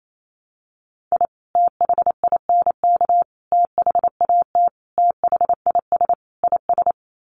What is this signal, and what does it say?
Morse code, heard as a keyed tone, about 28 wpm, 700 Hz. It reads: I THINK THAT THIS IS
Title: Sets of 5 Words Encore2 28wpm